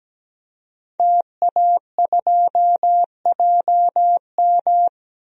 Morse code, heard as a keyed tone, about 17 wpm, 700 Hz. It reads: TA2JM
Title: DXCC-100 Call Signs - 5 Characters - 17wpm